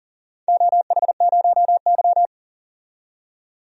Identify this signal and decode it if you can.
Morse code, heard as a keyed tone, about 40 wpm, 700 Hz. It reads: OH0Y
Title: DXCC-100 Call Signs - 4 Characters - 40wpm